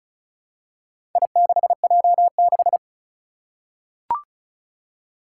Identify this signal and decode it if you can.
Morse code, heard as a keyed tone, about 36 wpm, 700 Hz. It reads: I6J6
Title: Four Challenging Letter-Number Combinations 35wpm